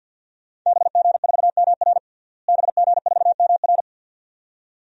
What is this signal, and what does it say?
Morse code, heard as a keyed tone, about 52 wpm, 700 Hz. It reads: BG4MR BG4MR
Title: DXCC-100 Call Signs - 5 Characters - Repeated 2x - 50wpm